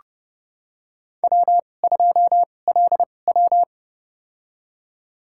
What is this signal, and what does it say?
Morse code, heard as a keyed tone, about 30 wpm, spaced as for 22 wpm, 700 Hz. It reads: W2LW
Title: USA Call Signs - 4 Characters - 2x Character Spacing - 30wpm